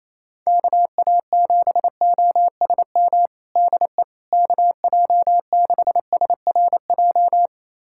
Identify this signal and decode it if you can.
Morse code, heard as a keyed tone, about 28 wpm, 700 Hz. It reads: KA7OSM DE KJ6SRJ
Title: USA Callsign DE Callsign 28wpm